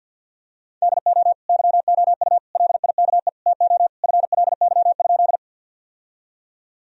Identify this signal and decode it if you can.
Morse code, heard as a keyed tone, about 50 wpm, 700 Hz. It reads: DO YOU LIKE TO FLY?
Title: Sentences from Top 1000 Words - Review Part 2 - 50wpm